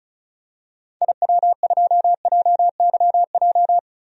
Morse code, heard as a keyed tone, about 34 wpm, 700 Hz. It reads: IW2JYJ
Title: DXCC-100 Call Signs - 6 Characters - 35wpm